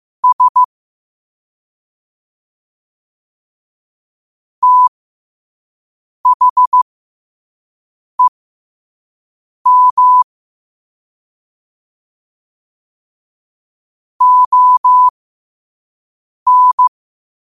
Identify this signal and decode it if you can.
Morse code, heard as a keyed tone, about 15 wpm, spaced as for 5 wpm, 1000 Hz. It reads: S THEM ON
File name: qotdfiles / 5wpm